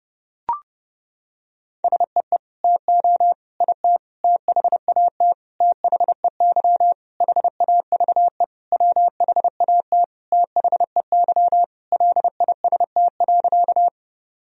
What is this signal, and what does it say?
Morse code, heard as a keyed tone, about 30 wpm, 700 Hz. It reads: SEE TO IT THAT THEY HAVE WHAT THEY LIST.
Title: Sentences from Top 1000 Words - Review Part 2 - 30wpm